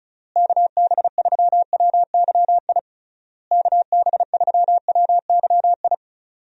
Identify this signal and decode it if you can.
Morse code, heard as a keyed tone, about 34 wpm, 700 Hz. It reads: KB3WYI KB3WYI
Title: USA Call Signs - 6 Characters - Repeated 3x - 35wpm